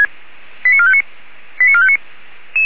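Five-tone selective call (ZVEI), heard as quick runs of five tones, tones of about 70 ms. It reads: 89459 89459
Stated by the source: ZVEI five-tone call 89233 received on an SDR, logged by the Orlamünde fire brigade (feuerwehr-orlamuende).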